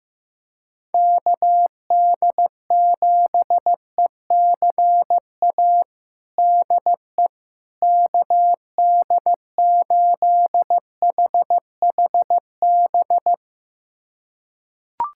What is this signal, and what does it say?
Morse code, heard as a keyed tone, about 15 wpm, 700 Hz. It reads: KD7ECA DE KD8HHB
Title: USA Callsign DE Callsign - 6 Characters - 15wpm